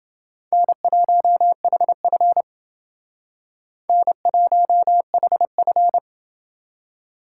Morse code, heard as a keyed tone, about 28 wpm, 700 Hz. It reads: N1HF N1HF